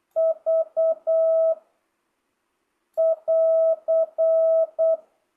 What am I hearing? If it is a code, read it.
Morse code, heard as a keyed tone, about 8 wpm, 660 Hz. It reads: V <AR>
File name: jh8cbh / 2025cw